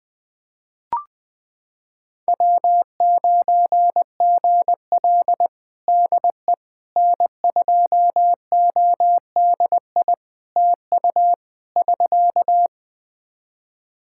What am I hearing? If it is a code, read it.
Morse code, heard as a keyed tone, about 20 wpm, 700 Hz. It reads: W9GL DE N2ODI TU <SK>
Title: Practice QSO - Exchange 4 - Ending - 20wpm